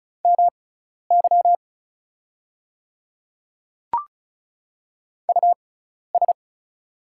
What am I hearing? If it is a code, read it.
Morse code, heard as a keyed tone, about 36 wpm, spaced as for 12 wpm, 700 Hz. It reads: MY US
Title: 6x Character Spacing - Top 500 Words - 35wpm